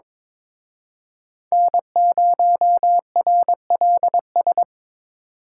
Morse code, heard as a keyed tone, about 26 wpm, 700 Hz. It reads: N0RLS
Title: Speed Racing - USA Call Signs 22wpm